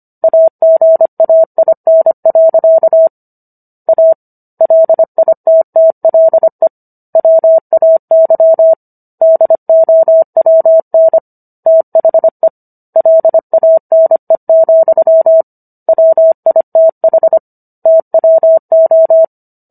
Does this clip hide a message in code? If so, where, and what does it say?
Yes: Morse code from 0.2 to 19.2 s, heard as a keyed tone, about 25 wpm, 650 Hz. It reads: AGAIN. A LITTLE WAY DOWN THE LANE, WITH TWO